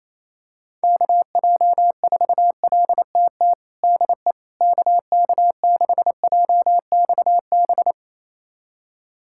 Morse code, heard as a keyed tone, about 28 wpm, 700 Hz. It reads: KJ4LTT DE KK6JXB
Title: USA Callsign DE Callsign 28wpm